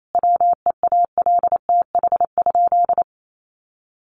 Morse code, heard as a keyed tone, about 28 wpm, 700 Hz. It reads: WEALTH?
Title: Sentences from Top 1200 Words 28wpm